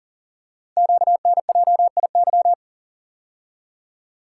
Morse code, heard as a keyed tone, about 41 wpm, 700 Hz. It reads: QNJIY
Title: Code Groups - 5 Letters - 40wpm